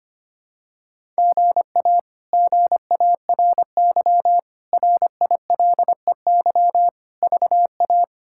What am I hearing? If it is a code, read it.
Morse code, heard as a keyed tone, about 25 wpm, 700 Hz. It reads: GA GARY RILEY VA
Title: K1USN SST QSO Practice - 25wpm